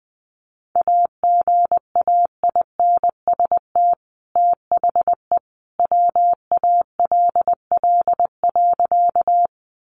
Morse code, heard as a keyed tone, about 20 wpm, 700 Hz. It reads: AGAINST THE WALL.
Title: Sentences from Top 1100 Words 20wpm